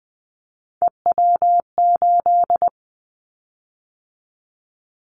Morse code, heard as a keyed tone, about 20 wpm, 700 Hz. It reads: EW8
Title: Callsign Prefixes for Top 50 DXCC - 20wpm